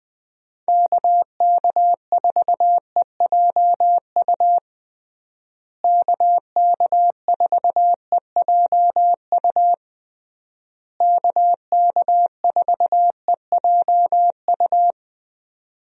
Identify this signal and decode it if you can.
Morse code, heard as a keyed tone, about 20 wpm, 700 Hz. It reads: KK4EJU KK4EJU KK4EJU
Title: USA Call Signs - 6 Characters - Repeated 3x - 20wpm